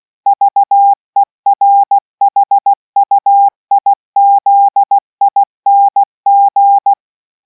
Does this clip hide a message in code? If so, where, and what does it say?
Yes: Morse code from 0.3 to 6.9 s, heard as a keyed tone, about 16 wpm, 800 Hz. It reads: VERHUIZING